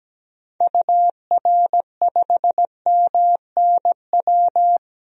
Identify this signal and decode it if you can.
Morse code, heard as a keyed tone, about 17 wpm, 700 Hz. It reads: UR5MNW
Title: DXCC-100 Call Signs - 6 Characters - 17wpm